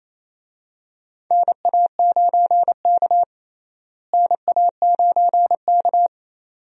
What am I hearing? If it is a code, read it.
Morse code, heard as a keyed tone, about 28 wpm, 700 Hz. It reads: NA9K NA9K
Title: USA Call Signs - 4 Characters - Repeated 2x - 28wpm